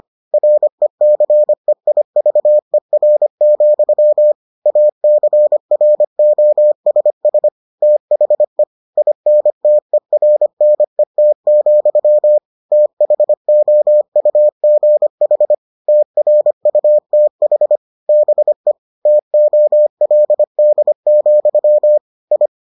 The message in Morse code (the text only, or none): RECEIVER, ACROSS THE INTERNET, THOUGH TRUTH BE TOLD, I